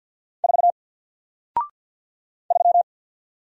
Morse code, heard as a keyed tone, about 49 wpm, 700 Hz. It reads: 4 3
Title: Single Letter-Number Alt 50wpm